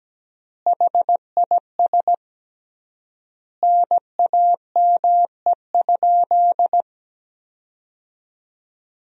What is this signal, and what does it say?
Morse code, heard as a keyed tone, about 17 wpm, 700 Hz. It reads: HIS NAME?
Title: 3x Word-Spacing - Sentences from Top 200 Words - 17wpm